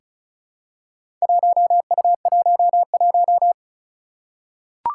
Morse code, heard as a keyed tone, about 36 wpm, 700 Hz. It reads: 1U11
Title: Four Challenging Letter-Number Combinations 35wpm